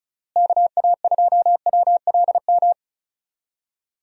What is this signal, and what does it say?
Morse code, heard as a keyed tone, about 36 wpm, 700 Hz. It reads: KA2WLM
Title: USA Call Signs - 6 Characters - Repeated 3x - 35wpm